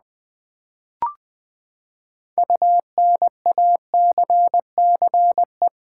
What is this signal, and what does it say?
Morse code, heard as a keyed tone, about 20 wpm, 700 Hz. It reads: UNACCE
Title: Build A Word - 6 Letters At A Time - 12-Letter Words - 20wpm